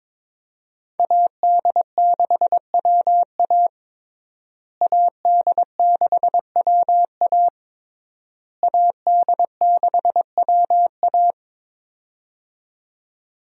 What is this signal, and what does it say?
Morse code, heard as a keyed tone, about 22 wpm, 700 Hz. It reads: AD6WA AD6WA AD6WA
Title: USA Call Signs - 5 Characters - Repeated 3x - 22wpm